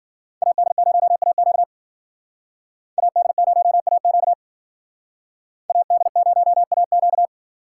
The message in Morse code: AD0AQ AD0AQ AD0AQ

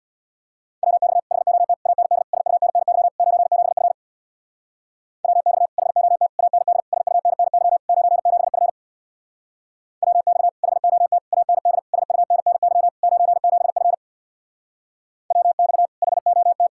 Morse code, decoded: WX HOT AND SUNNY 97F WX HOT AND SUNNY 97F WX HOT AND SUNNY 97F WX HOT

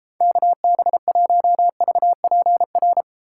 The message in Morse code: KB1VPR

